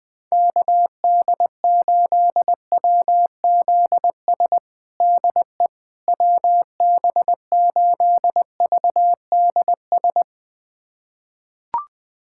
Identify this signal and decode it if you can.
Morse code, heard as a keyed tone, about 20 wpm, 700 Hz. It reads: KD8WZS DE WB8VDS
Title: USA Callsign DE Callsign - 6 Characters - 20wpm